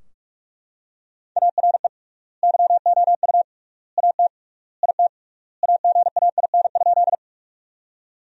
Morse code, heard as a keyed tone, about 41 wpm, 700 Hz. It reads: ARE YOU AT IT AGAIN?
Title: 3x Word-Spacing - Sentences from Top 200 Words - 45wpm